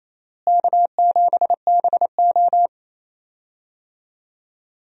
Morse code, heard as a keyed tone, about 28 wpm, 700 Hz. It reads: K7BO
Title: USA Call Signs - 4 Characters - Repeated 3x - 28wpm